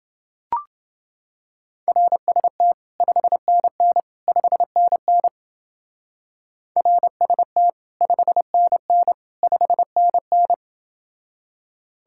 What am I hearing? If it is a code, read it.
Morse code, heard as a keyed tone, about 30 wpm, 700 Hz. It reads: RST 5NN 5NN RST 5NN 5NN